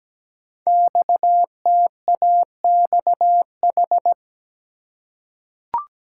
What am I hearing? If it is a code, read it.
Morse code, heard as a keyed tone, about 17 wpm, 700 Hz. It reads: XTAXH
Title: Code Groups - 5 Letters - 17wpm